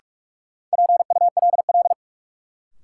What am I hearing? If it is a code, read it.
Morse code, heard as a keyed tone, about 46 wpm, 700 Hz. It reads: PULL
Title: Sentences from Top 400 Words - New Word Drill - 45wpm